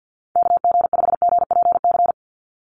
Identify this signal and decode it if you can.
Morse code, heard as a keyed tone, about 50 wpm, 700 Hz. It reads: KG5GPC
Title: USA Call Signs 50wpm